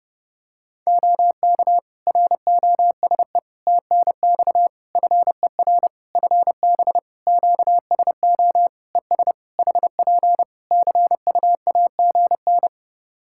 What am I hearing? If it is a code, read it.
Morse code, heard as a keyed tone, about 30 wpm, 700 Hz. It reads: OK ROSE TNX FER FB QSO ES HP CUAGN
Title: Practice QSO - Exchange 4 - Ending - 30wpm